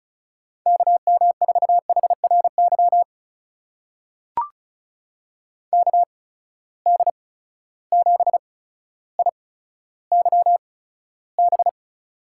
Morse code, heard as a keyed tone, about 36 wpm, spaced as for 15 wpm, 700 Hz. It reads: KM4HRY KD7IYB
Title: USA Call Signs - 6 Characters - 8x Character Spacing - 35wpm